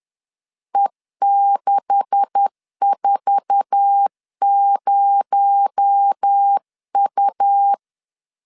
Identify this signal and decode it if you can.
Morse code, heard as a keyed tone, about 10 wpm, 800 Hz. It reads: E640U
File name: LEZIONI / 10 SIGNS / 10A SLASH 10wpm 2808